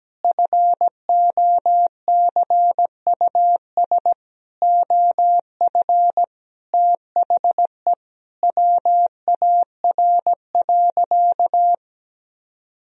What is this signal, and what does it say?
Morse code, heard as a keyed tone, about 17 wpm, 700 Hz. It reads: FOCUS OF THE WAR.
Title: Sentences from Top 1000 Words - Review Part 4 - 17wpm